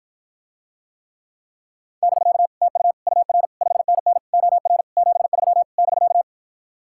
Morse code, heard as a keyed tone, about 53 wpm, 700 Hz. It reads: <BK> TU UR 5NN OR 73 <BK>